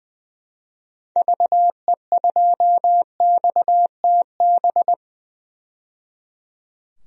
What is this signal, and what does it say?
Morse code, heard as a keyed tone, about 20 wpm, 700 Hz. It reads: VE2XTB
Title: DXCC-100 Call Signs - 6 Characters - 20wpm